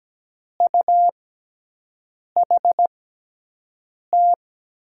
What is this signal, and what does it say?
Morse code, heard as a keyed tone, about 17 wpm, spaced as for 6 wpm, 700 Hz. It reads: UHT
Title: USA Call Signs - 6 Characters - 6x Character Spacing - 17wpm